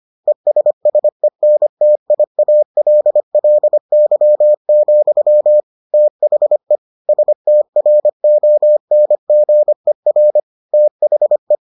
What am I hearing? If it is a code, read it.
Morse code, heard as a keyed tone, about 25 wpm, 600 Hz. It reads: ESSENTIALLY, THE STRONGER THE